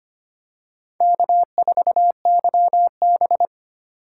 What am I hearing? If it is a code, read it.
Morse code, heard as a keyed tone, about 25 wpm, 700 Hz. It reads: K4YB
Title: USA Call Signs - 4 Characters - 4x Character Spacing - 25wpm